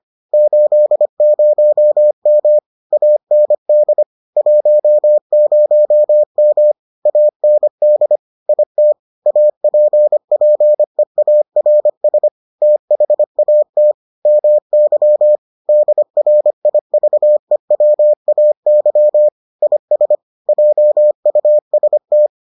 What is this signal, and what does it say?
Morse code, heard as a keyed tone, about 25 wpm, 600 Hz. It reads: 80M AND 10M AND IT APPEARS THAT MY DRIVEWAY IS JUST